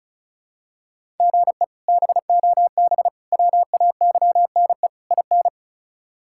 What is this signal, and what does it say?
Morse code, heard as a keyed tone, about 34 wpm, 700 Hz. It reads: GE BOB WAYNE IN
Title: K1USN SST QSO Practice - 35wpm